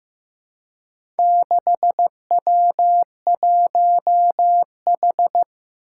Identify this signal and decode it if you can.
Morse code, heard as a keyed tone, about 15 wpm, 700 Hz. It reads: 6W1H
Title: Four Challenging Letter-Number Combinations 15wpm